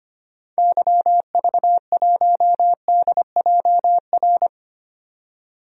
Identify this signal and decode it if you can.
Morse code, heard as a keyed tone, about 25 wpm, 700 Hz. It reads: YV1DJR
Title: DXCC-100 Call Signs - 6 Characters - 25wpm